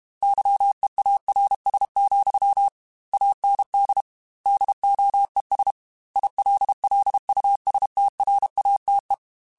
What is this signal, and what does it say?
Morse code, heard as a keyed tone, about 32 wpm, 800 Hz. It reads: YEARS, AND DOES ILLUSTRATE
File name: Technical Subjects / Technical Subjects-mp3s / AND Gate @30